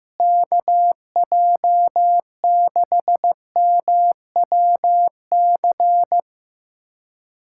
Morse code, heard as a keyed tone, about 16 wpm, 700 Hz. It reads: KJ6MWC
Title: USA Callsign DE Callsign 15wpm